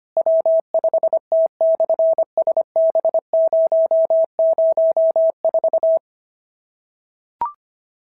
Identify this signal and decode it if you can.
Morse code, heard as a keyed tone, about 25 wpm, 650 Hz. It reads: W5T/SB004
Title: SOTA QSO Practice 25wpm